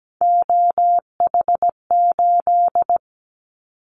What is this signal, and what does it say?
Morse code, heard as a keyed tone, about 17 wpm, 700 Hz. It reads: OH8
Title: Callsign Prefixes for Top 25 DXCC - 17wpm